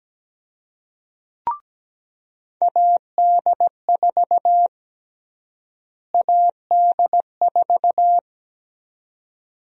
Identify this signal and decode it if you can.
Morse code, heard as a keyed tone, about 17 wpm, 700 Hz. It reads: AD4 AD4
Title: USA Call Sign Prefixes - Repeated 2x - 17wpm